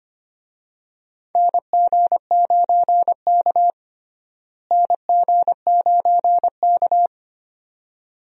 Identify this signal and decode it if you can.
Morse code, heard as a keyed tone, about 25 wpm, 700 Hz. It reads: NG9K NG9K
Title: USA Call Signs - 4 Characters - Repeated 2x - 25wpm